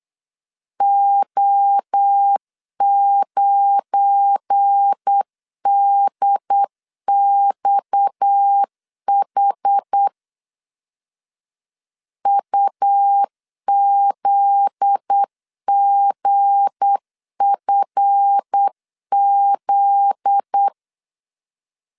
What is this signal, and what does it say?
Morse code, heard as a keyed tone, about 8 wpm, 800 Hz. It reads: O9DXH UZGFZ
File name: LEZIONI / 7 GF9Z / 7 GF9Z 8wpm 3213